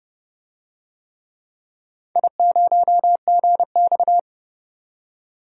Morse code, heard as a keyed tone, about 31 wpm, 700 Hz. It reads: I0GX
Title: DXCC-100 Call Signs - 4 Characters - 30wpm